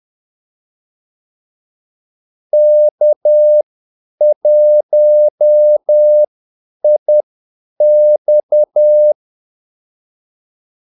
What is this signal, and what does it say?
Morse code, heard as a keyed tone, about 10 wpm, spaced as for 8 wpm, 600 Hz. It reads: K1IX
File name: Callsigns-10-8wpm